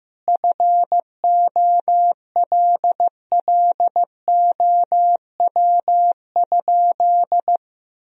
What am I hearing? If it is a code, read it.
Morse code, heard as a keyed tone, about 15 wpm, 700 Hz. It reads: FOLLOW?
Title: Sentences from Top 1000 Words - Review Part 3 - 15wpm